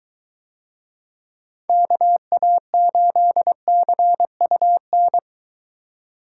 Morse code, heard as a keyed tone, about 21 wpm, 700 Hz. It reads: KA8CUN